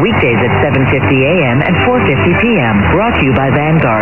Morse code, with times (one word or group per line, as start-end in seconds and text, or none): none